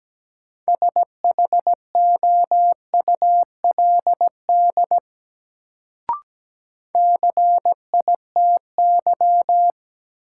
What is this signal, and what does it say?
Morse code, heard as a keyed tone, about 17 wpm, 700 Hz. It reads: SHOULD CITY
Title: Sentences from Top 300 Words - New Word Drill - 17wpm